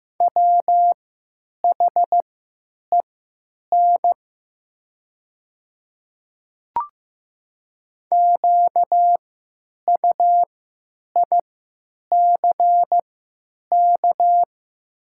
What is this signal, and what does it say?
Morse code, heard as a keyed tone, about 15 wpm, spaced as for 8 wpm, 700 Hz. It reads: WHEN QUICK
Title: 3x Character Spacing - Top 500 Words - 15wpm